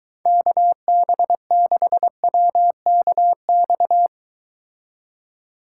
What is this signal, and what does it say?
Morse code, heard as a keyed tone, about 22 wpm, 700 Hz. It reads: KB6WKX